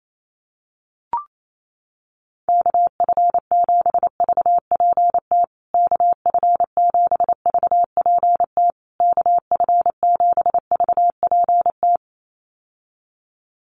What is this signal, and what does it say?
Morse code, heard as a keyed tone, about 28 wpm, 700 Hz. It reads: KF7VPT KF7VPT KF7VPT
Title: USA Call Signs Repeated 3x 28wpm